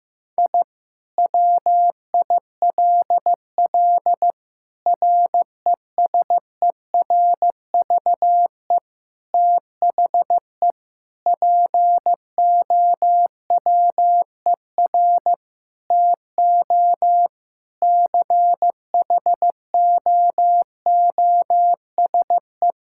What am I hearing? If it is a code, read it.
Morse code, heard as a keyed tone, about 15 wpm, 700 Hz. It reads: I WILL RESERVE THE POWER TO CHOOSE
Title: Sentences from Top 1000 Words - Review Part 6 - 15wpm